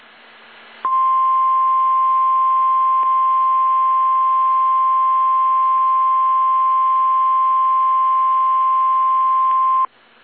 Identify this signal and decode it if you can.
Emergency Alert System (SAME) tones (the 1050 Hz attention tone). It is heard from 0.8 to 9.9 s.
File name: eas / KSDZKDJL / KSDZKDJL